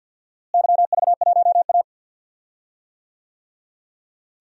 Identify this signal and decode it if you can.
Morse code, heard as a keyed tone, about 45 wpm, 700 Hz. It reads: YV1A
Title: DXCC-100 Call Signs - 4 Characters - Repeated 2x - 50wpm